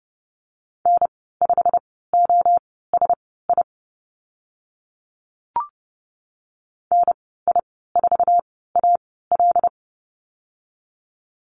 Morse code, heard as a keyed tone, about 32 wpm, spaced as for 17 wpm, 700 Hz. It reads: N5OSI NI4AL
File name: Lesson 15g - Number 5 - Call Signs 30wpm